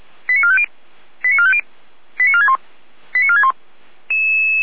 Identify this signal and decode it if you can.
Five-tone selective call (ZVEI), heard as quick runs of five tones, tones of about 70 ms. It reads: 89450 89450 89461 89461
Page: feuerwehr-orlamuende